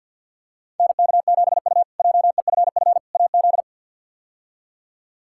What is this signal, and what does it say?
Morse code, heard as a keyed tone, about 52 wpm, 700 Hz. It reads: NK7U JEFF AZ
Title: K1USN SST QSO Practice - 50wpm